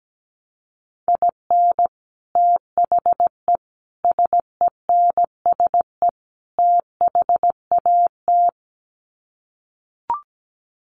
Morse code, heard as a keyed tone, about 17 wpm, 700 Hz. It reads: IN THE SENSE THAT
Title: Sets of 4 Words 17wpm